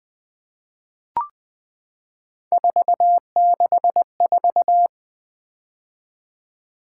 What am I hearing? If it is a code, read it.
Morse code, heard as a keyed tone, about 20 wpm, 700 Hz. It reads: 464
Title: Three Letter-Number Combinations - 20wpm